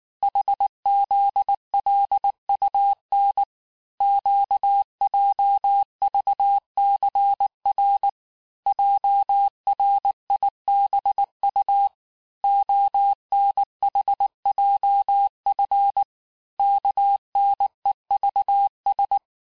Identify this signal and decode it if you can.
Morse code, heard as a keyed tone, about 19 wpm, 800 Hz. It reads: HZLUN QJVCR JRIBU ONHJF KNEVS
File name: cours-du-mercredi / audios / cours-869-texte-CHIFFRE-vitesse-18-MPM